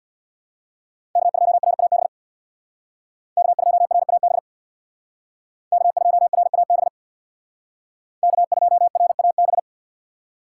Morse code, heard as a keyed tone, about 61 wpm, 700 Hz. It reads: K2RAB K2RAB K2RAB K2RAB
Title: Speed Racing - USA 1x3 Call Signs 50wpm